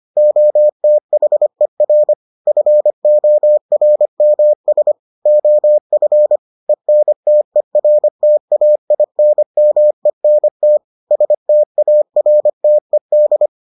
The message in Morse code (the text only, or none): OTHER FORMS OF ENTERTAINMENT STARTED